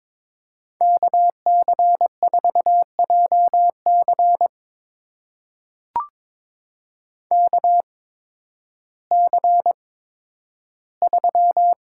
Morse code, heard as a keyed tone, about 22 wpm, spaced as for 12 wpm, 700 Hz. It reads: KC4JC KC3
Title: USA Call Signs - 8x Character Spacing - 22wpm